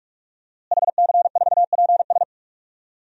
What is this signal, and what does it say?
Morse code, heard as a keyed tone, about 46 wpm, 700 Hz. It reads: HC4PS
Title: DXCC-100 Call Signs - 5 Characters - Repeated 3x - 45wpm